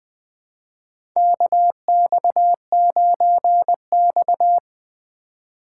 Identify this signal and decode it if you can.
Morse code, heard as a keyed tone, about 20 wpm, 700 Hz. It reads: KX9X
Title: USA Call Signs - 4x Character Spacing - 20wpm